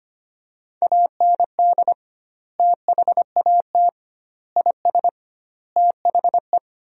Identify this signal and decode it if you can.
Morse code, heard as a keyed tone, about 25 wpm, 700 Hz. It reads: AND THAT IS THE
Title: 2x Word-Spacing - Sentences from Top 400 Words - 25wpm